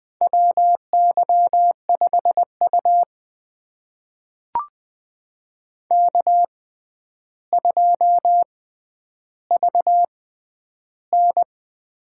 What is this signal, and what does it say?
Morse code, heard as a keyed tone, about 20 wpm, spaced as for 10 wpm, 700 Hz. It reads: WY5U K2VN
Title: USA Call Signs - 4 Characters - 6x Character Spacing - 20wpm